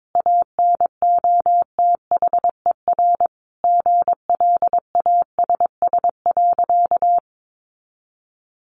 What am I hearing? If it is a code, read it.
Morse code, heard as a keyed tone, about 22 wpm, 700 Hz. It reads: ANOTHER GLASS.
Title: Sentences from Top 1500 Words 22wpm